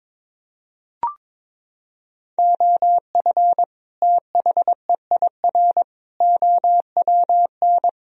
Morse code, heard as a keyed tone, about 22 wpm, 700 Hz. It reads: OF THEIR OWN
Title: Sentences from Top 1000 Words - Review Part 3 - 22wpm